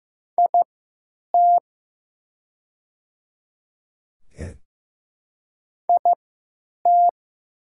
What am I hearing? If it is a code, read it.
Morse code, heard as a keyed tone, about 15 wpm, spaced as for 9 wpm, 690 Hz. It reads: IT IT
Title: 3x Character Spacing - Top 300 Words - 15wpm